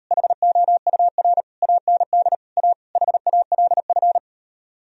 Morse code, heard as a keyed tone, about 39 wpm, 700 Hz. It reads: HOUR AND A HALF